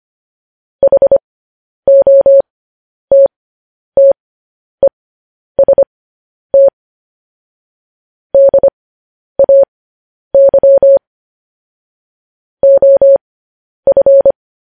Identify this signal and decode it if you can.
Morse code, heard as a keyed tone, about 25 wpm, spaced as for 10 wpm, 550 Hz. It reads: HOTTEST DAY OF
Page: news headlines in morse code at 10 wpm